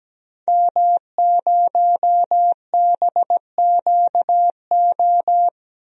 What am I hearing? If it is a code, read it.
Morse code, heard as a keyed tone, about 17 wpm, 700 Hz. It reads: M0BQO
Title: DXCC-100 Call Signs - 5 Characters - Repeated 2x - 17wpm